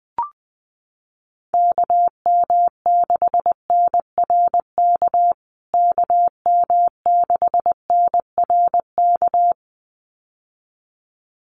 Morse code, heard as a keyed tone, about 19 wpm, 700 Hz. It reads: KM6NRK KM6NRK
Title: USA Call Signs Repeated 2x 20wpm